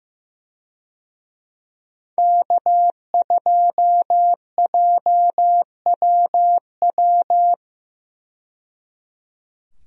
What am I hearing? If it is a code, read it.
Morse code, heard as a keyed tone, about 15 wpm, 700 Hz. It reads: K2JWW
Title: USA Call Signs - 5 Characters - 15wpm